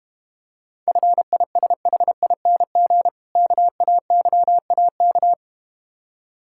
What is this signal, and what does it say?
Morse code, heard as a keyed tone, about 32 wpm, 700 Hz. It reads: FISHING KAYAK